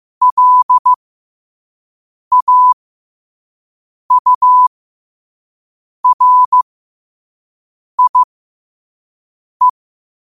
Morse code, heard as a keyed tone, about 15 wpm, spaced as for 5 wpm, 1000 Hz. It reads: LAURIE
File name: qotdfiles / 5wpm